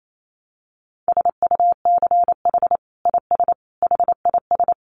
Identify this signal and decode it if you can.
Morse code, heard as a keyed tone, about 28 wpm, 700 Hz. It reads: SUCH IS HIS
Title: Sentences from Top 900 Words 28wpm